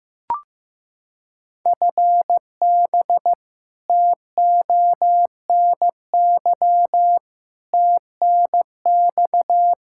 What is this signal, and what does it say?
Morse code, heard as a keyed tone, about 15 wpm, 700 Hz. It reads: FB TONY TNX
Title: Practice QSO - Exchange 3 - More Info - 15wpm